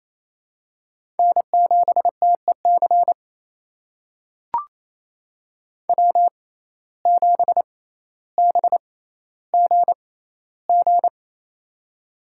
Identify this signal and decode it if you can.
Morse code, heard as a keyed tone, about 28 wpm, spaced as for 14 wpm, 700 Hz. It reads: N7TEC W7BGG
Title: USA Call Signs - 6x Character Spacing - 28wpm